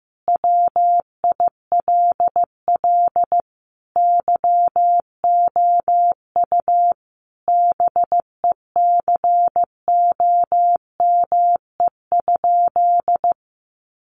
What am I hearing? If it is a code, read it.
Morse code, heard as a keyed tone, about 15 wpm, 700 Hz. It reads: WILL YOU BECOME?
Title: Sentences from Top 600 Words 15wpm